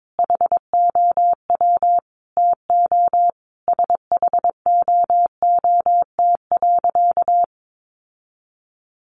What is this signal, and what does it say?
Morse code, heard as a keyed tone, about 22 wpm, 700 Hz. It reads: HOW TO SHOOT.
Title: Sentences from Top 1200 Words 22wpm